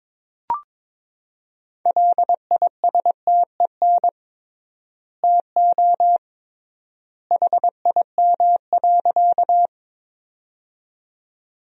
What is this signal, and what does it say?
Morse code, heard as a keyed tone, about 22 wpm, 700 Hz. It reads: LISTEN TO HIM.
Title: 3x Word-Spacing - Sentences from Top 400 Words - 22wpm